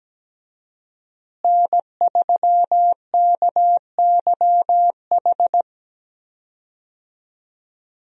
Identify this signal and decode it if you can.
Morse code, heard as a keyed tone, about 17 wpm, 700 Hz. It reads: N3KYH
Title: USA Call Signs - 5 Characters - Repeated 2x - 17wpm